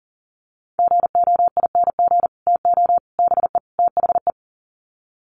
Morse code, heard as a keyed tone, about 40 wpm, 700 Hz. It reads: GOING TO BE THE
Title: Sets of 4 Words 40wpm